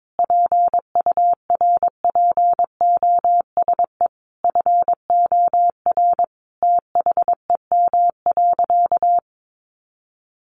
Morse code, heard as a keyed tone, about 22 wpm, 700 Hz. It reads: PURPOSE FOR THEM.
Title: Sentences from Top 800 Words 22wpm